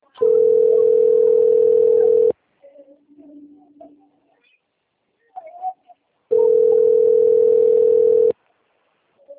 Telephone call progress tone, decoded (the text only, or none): ringback tone